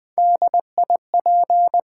Morse code, heard as a keyed tone, about 20 wpm, 700 Hz. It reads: DIP
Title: Top 90 QSO Elements - 8x Character Spacing - 20wpm